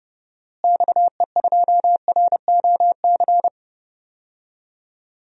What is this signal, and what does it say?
Morse code, heard as a keyed tone, about 30 wpm, 700 Hz. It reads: XE2ROC
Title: DXCC-100 Call Signs - 6 Characters - Repeated 2x - 30wpm